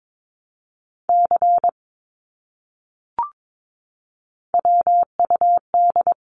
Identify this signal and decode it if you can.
Morse code, heard as a keyed tone, about 22 wpm, 700 Hz. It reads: C WUD
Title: Abbreviations Repeated 3x - 22wpm